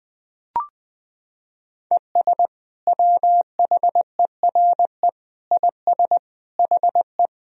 Morse code, heard as a keyed tone, about 20 wpm, 700 Hz. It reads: ES WHERE IS HE
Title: Sentences from Top 200 Words and Abbreviations 20wpm